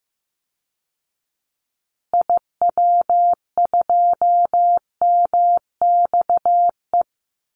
Morse code, heard as a keyed tone, about 15 wpm, 700 Hz. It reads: IW2MXE
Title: Call Signs Repeated 2x - 15wpm